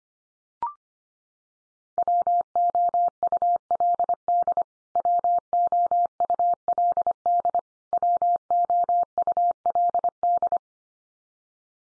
Morse code, heard as a keyed tone, about 25 wpm, 700 Hz. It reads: WOULD WOULD WOULD
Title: Top 100 Words 25wpm x3